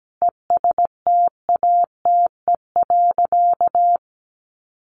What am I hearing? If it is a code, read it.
Morse code, heard as a keyed tone, about 17 wpm, 700 Hz. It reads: ESTATE.
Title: Sentences from Top 1200 Words 17wpm